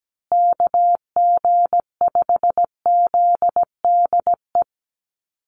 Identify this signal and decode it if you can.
Morse code, heard as a keyed tone, about 17 wpm, 700 Hz. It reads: KG5ZDE
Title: USA Call Signs Repeated 3x 17wpm